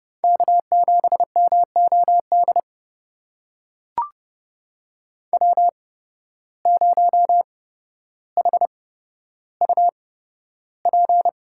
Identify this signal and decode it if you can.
Morse code, heard as a keyed tone, about 30 wpm, spaced as for 13 wpm, 700 Hz. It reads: K7MOD W0HUP
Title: USA Call Signs - 5 Characters - 8x Character Spacing - 30wpm